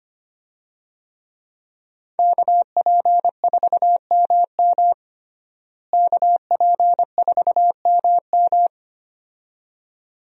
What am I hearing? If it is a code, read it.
Morse code, heard as a keyed tone, about 25 wpm, 700 Hz. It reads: KP4MM KP4MM
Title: USA Call Signs - 5 Characters - Repeated 2x - 25wpm